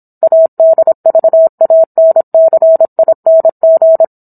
Morse code, heard as a keyed tone, about 27 wpm, 650 Hz. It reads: ADVANCING